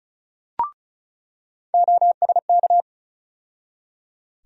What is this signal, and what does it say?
Morse code, heard as a keyed tone, about 36 wpm, 700 Hz. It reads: OSK
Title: USA Call Sign Postfixes - 3 Letters 35wpm